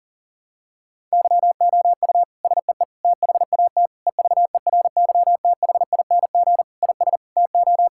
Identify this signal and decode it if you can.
Morse code, heard as a keyed tone, about 40 wpm, 700 Hz. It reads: YOU SEE THAT EVERYTHING IS TO
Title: Sentences from Top 1000 Words - Review Part 6 - 40wpm